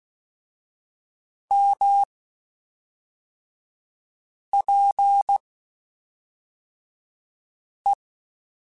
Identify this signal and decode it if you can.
Morse code, heard as a keyed tone, about 16 wpm, 800 Hz. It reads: M P E
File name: Spaced Characters / Spaced Characters-mp3s / Weather 4 10 sp @5